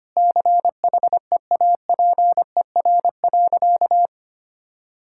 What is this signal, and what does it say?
Morse code, heard as a keyed tone, about 25 wpm, 700 Hz. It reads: CHEAPER.
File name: practice / 25wpm / Difficult Sentences with Hints - 25wpm